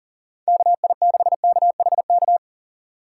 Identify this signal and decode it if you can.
Morse code, heard as a keyed tone, about 40 wpm, 700 Hz. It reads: KI6KHK
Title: USA Call Signs - 6 Characters - 4x Character Spacing - 40wpm